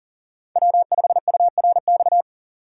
Morse code, heard as a keyed tone, about 40 wpm, 700 Hz. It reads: W5URX
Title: USA Call Signs - 5 Characters - 3x Character Spacing - 40wpm